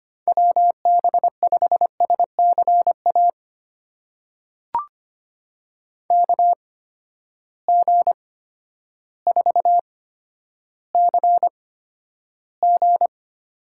Morse code, heard as a keyed tone, about 25 wpm, spaced as for 11 wpm, 700 Hz. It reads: WB5SCA KG4CG
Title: USA Call Signs - 6 Characters - 8x Character Spacing - 25wpm